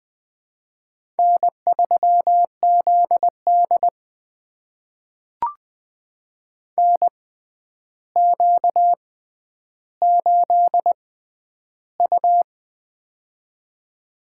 Morse code, heard as a keyed tone, about 19 wpm, spaced as for 10 wpm, 700 Hz. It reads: N3ZD NQ8U
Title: USA Call Signs - 4 Characters - 6x Character Spacing - 20wpm